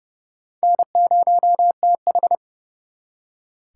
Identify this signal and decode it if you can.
Morse code, heard as a keyed tone, about 36 wpm, 700 Hz. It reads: N0TH
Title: Speed Racing - USA 1x2 Call Signs 30wpm